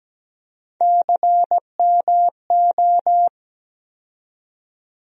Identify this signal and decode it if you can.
Morse code, heard as a keyed tone, about 17 wpm, 700 Hz. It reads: CMO
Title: USA Call Sign Postfixes - 3 Letters 17wpm